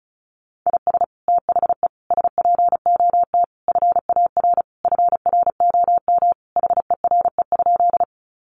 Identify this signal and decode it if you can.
Morse code, heard as a keyed tone, about 34 wpm, 700 Hz. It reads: IS THE SPOT FAR FROM HERE?
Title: Sentences from Top 1100 Words 35wpm